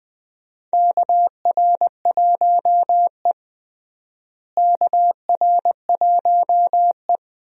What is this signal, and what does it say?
Morse code, heard as a keyed tone, about 20 wpm, 700 Hz. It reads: KR1E KR1E
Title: USA Call Signs - 4 Characters - Repeated 2x - 20wpm